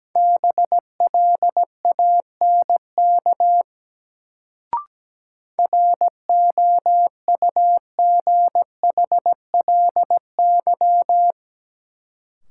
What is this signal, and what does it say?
Morse code, heard as a keyed tone, about 17 wpm, 700 Hz. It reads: BLANK ROUGHLY
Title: Sentences from Top 1500 Words - New Word Drill Alt - 17wpm